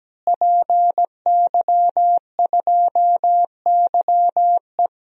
Morse code, heard as a keyed tone, about 17 wpm, 700 Hz. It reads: PY2YE
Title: DXCC-100 Call Signs - 5 Characters - Repeated 2x - 17wpm